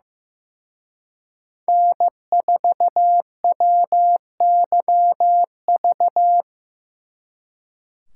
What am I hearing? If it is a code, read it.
Morse code, heard as a keyed tone, about 18 wpm, 700 Hz. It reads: N4WYV
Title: Speed Racing - USA Call Signs 15wpm